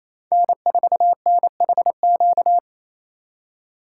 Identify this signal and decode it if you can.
Morse code, heard as a keyed tone, about 28 wpm, 700 Hz. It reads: N4NHQ
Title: USA Call Signs - 5 Characters - 2x Character Spacing - 28wpm